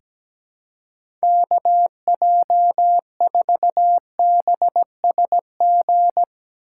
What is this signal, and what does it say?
Morse code, heard as a keyed tone, about 17 wpm, 700 Hz. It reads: KJ4BSG
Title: USA Call Signs - 6 Characters - 6x Character Spacing - 17wpm